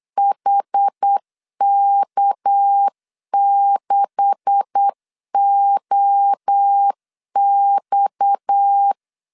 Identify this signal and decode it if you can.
Morse code, heard as a keyed tone, about 8 wpm, 800 Hz. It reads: HK6OX
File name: LEZIONI / 9 286 8wpm 3425